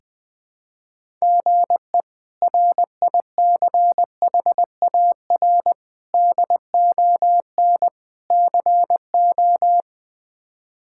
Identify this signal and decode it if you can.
Morse code, heard as a keyed tone, about 20 wpm, 700 Hz. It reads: GE RICHAR DON CO
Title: K1USN SST QSO Practice - 20wpm